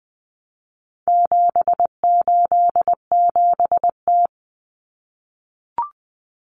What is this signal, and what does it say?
Morse code, heard as a keyed tone, about 20 wpm, 700 Hz. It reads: 787T
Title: Common Cut Numbers - 4-Digit 20wpm